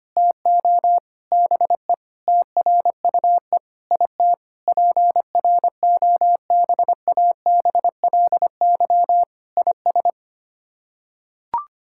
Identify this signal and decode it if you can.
Morse code, heard as a keyed tone, about 25 wpm, 700 Hz. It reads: TO BE TRUE IT PROBABLY IS
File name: Sets of 9 Words - 25wpm